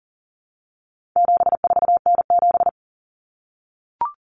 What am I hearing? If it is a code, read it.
Morse code, heard as a keyed tone, about 40 wpm, 700 Hz. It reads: 74N7
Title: Common Cut Numbers - 4-Digit 40wpm